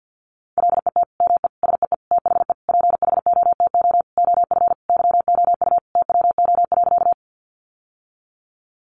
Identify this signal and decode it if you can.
Morse code, heard as a keyed tone, about 50 wpm, 700 Hz. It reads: LET ME SEE THE PHOTO OF YOU TWO.
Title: Sentences from Top 1200 Words 50wpm